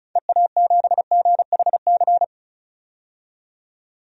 Morse code, heard as a keyed tone, about 36 wpm, 700 Hz. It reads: EA7GHC
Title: DXCC-100 Call Signs - 6 Characters - 35wpm